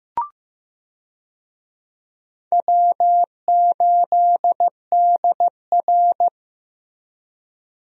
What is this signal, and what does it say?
Morse code, heard as a keyed tone, about 16 wpm, 700 Hz. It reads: W8DR
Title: USA Call Signs - 4 Characters - Repeated 2x - 15wpm